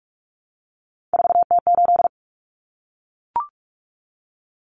Morse code, heard as a keyed tone, about 44 wpm, 700 Hz. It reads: 4T8
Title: Common Cut Numbers - 3-Digit 45wpm